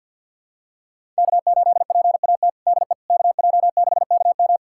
Morse code, heard as a keyed tone, about 52 wpm, 700 Hz. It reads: K8PAT DE KJ6KM
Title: USA Callsign DE Callsign - 5 Characters - 50wpm